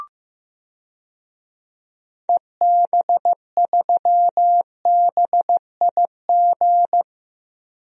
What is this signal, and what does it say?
Morse code, heard as a keyed tone, about 15 wpm, 700 Hz. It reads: EB3BIG
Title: DXCC-100 Call Signs - 6 Characters - 15wpm